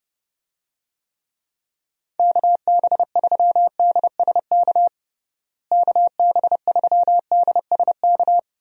Morse code, heard as a keyed tone, about 30 wpm, 700 Hz. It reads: KB3DSK KB3DSK
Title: USA Call Signs - 6 Characters - Repeated 2x - 30wpm